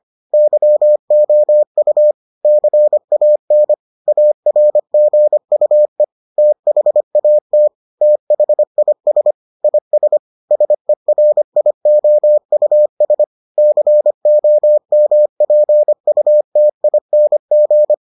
Morse code, heard as a keyed tone, about 25 wpm, 600 Hz. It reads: YOU CAN ARGUE THAT THIS IS SERIOUS COMPUTING